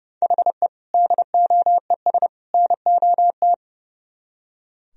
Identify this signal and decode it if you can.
Morse code, heard as a keyed tone, about 38 wpm, 700 Hz. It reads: HE DOES NOT